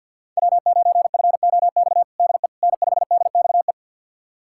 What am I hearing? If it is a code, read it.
Morse code, heard as a keyed tone, about 52 wpm, 700 Hz. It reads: W9FOX DE N5DXE
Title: USA Callsign DE Callsign - 5 Characters - 50wpm